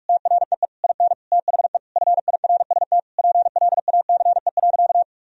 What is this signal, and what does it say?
Morse code, heard as a keyed tone, about 41 wpm, 700 Hz. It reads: TREE IN THE FIRST PLACE.
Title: Sentences from Top 1000 Words - Review Part 1 - 45wpm